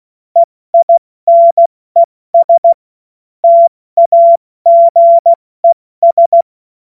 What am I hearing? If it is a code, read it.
Morse code, heard as a keyed tone, about 16 wpm, spaced as for 14 wpm, 680 Hz. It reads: EINES TAGES
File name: Morsekurs / Witze / Witz 20. September 2022 70 F80